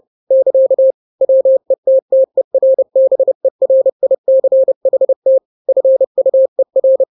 Morse code, heard as a keyed tone, about 30 wpm, 530 Hz. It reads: <KA> WETTERBERICHT FUER
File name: daily / 29wpm